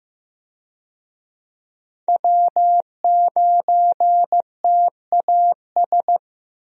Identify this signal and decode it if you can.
Morse code, heard as a keyed tone, about 15 wpm, 700 Hz. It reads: W9TAS
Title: USA Call Signs - 5 Characters - 15wpm